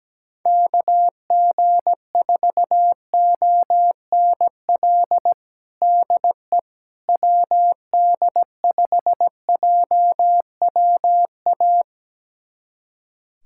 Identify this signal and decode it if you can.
Morse code, heard as a keyed tone, about 17 wpm, 700 Hz. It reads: KG4ONL DE WD5JWA
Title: USA Callsign DE Callsign - 6 Characters - 17wpm